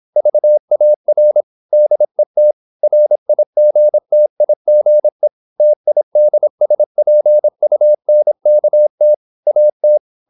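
Morse code, heard as a keyed tone, about 26 wpm, 600 Hz. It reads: VAR DET RIGTIGE TIDSPUNKT AT
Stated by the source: High-speed (QRQ) Morse at 26 wpm.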